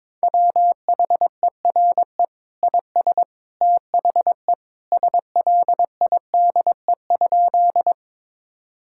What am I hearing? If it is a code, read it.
Morse code, heard as a keyed tone, about 22 wpm, 700 Hz. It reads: WHERE IS THE SLIDE?
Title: Sentences from Top 1600 Words 22wpm